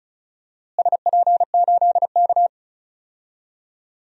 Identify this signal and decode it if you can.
Morse code, heard as a keyed tone, about 36 wpm, 700 Hz. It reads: SP8K
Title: DXCC-100 Call Signs - 4 Characters - Repeated 2x - 35wpm